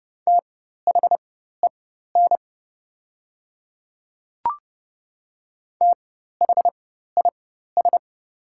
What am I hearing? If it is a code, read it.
Morse code, heard as a keyed tone, about 30 wpm, spaced as for 14 wpm, 700 Hz. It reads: THEN THIS
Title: 4x Character Spacing - Top 200 Words - 30wpm